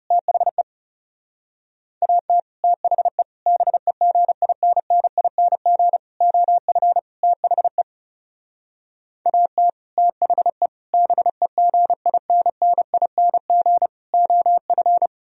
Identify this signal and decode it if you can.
Morse code, heard as a keyed tone, about 40 wpm, 700 Hz. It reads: THE AT THE BEGINNING OF THE AT THE BEGINNING OF